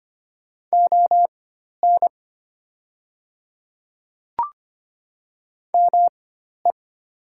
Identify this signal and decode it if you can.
Morse code, heard as a keyed tone, about 25 wpm, spaced as for 12 wpm, 700 Hz. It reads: ON ME
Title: 4x Character Spacing - Top 200 Words - 25wpm